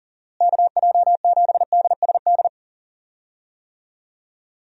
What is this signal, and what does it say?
Morse code, heard as a keyed tone, about 40 wpm, 700 Hz. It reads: KJ7DSD
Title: USA Call Signs - 6 Characters - Repeated 3x - 40wpm